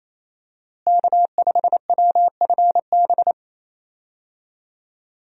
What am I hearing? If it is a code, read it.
Morse code, heard as a keyed tone, about 28 wpm, 700 Hz. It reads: K5WFB
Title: USA Call Signs - 5 Characters - Repeated 2x - 28wpm